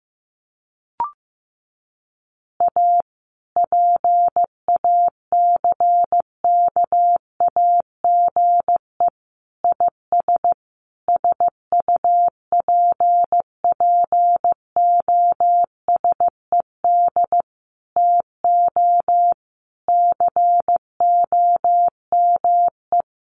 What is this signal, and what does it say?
Morse code, heard as a keyed tone, about 15 wpm, 700 Hz. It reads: A PACKAGE IS SUPPOSED TO COME
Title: Additional Easy Phrases 15wpm